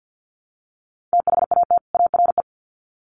Morse code, heard as a keyed tone, about 50 wpm, 700 Hz. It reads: THAT ARE
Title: Sets of 2 Words 50wpm